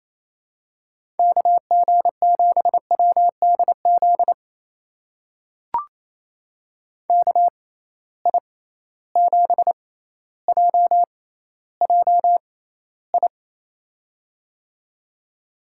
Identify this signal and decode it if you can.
Morse code, heard as a keyed tone, about 27 wpm, spaced as for 14 wpm, 700 Hz. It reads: KG7WDZ KI7JJI
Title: USA Call Signs - 6 Characters - 6x Character Spacing - 28wpm